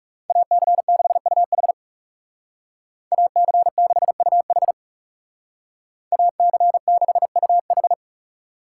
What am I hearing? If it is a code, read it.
Morse code, heard as a keyed tone, about 35 wpm, 700 Hz. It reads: AC6UH AC6UH AC6UH
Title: Speed Racing - USA Call Signs 30wpm